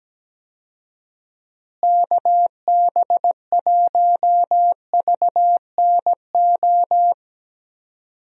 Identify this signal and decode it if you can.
Morse code, heard as a keyed tone, about 17 wpm, 700 Hz. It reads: KB1VNO
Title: USA Call Signs - 6 Characters - Repeated 3x - 17wpm